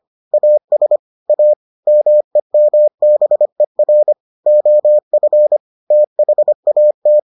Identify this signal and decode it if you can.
Morse code, heard as a keyed tone, about 25 wpm, 600 Hz. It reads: AS A MEMBER OF THAT